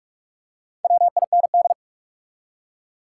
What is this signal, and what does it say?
Morse code, heard as a keyed tone, about 43 wpm, 700 Hz. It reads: WIND
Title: Four Letter Words - 45wpm